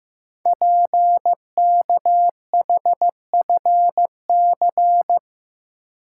Morse code, heard as a keyed tone, about 15 wpm, 700 Hz. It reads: PKHFC
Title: Code Groups - 5 Letters - 15wpm